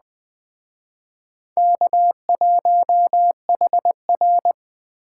Morse code, heard as a keyed tone, about 20 wpm, 700 Hz. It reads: K1HR